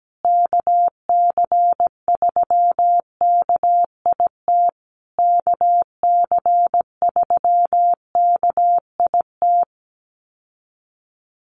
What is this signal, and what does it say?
Morse code, heard as a keyed tone, about 17 wpm, 700 Hz. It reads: KC3KIT KC3KIT
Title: USA Call Signs Repeated 3x 17wpm